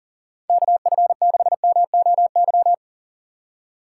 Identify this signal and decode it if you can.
Morse code, heard as a keyed tone, about 40 wpm, 700 Hz. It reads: KF6MOY
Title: USA Call Signs - 6 Characters - 4x Character Spacing - 40wpm